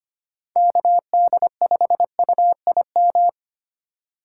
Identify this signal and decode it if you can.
Morse code, heard as a keyed tone, about 25 wpm, 700 Hz. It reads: KD5UIM
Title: USA Call Signs - 2x Character Spacing - 25wpm